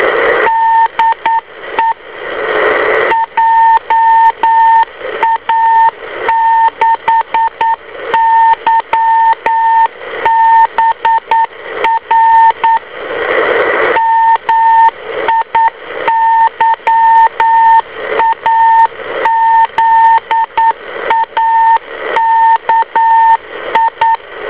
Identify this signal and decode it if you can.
Morse code, heard as a keyed tone, about 9 wpm, 910 Hz, with radio noise. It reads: DE JA6YBR MIYAZAKI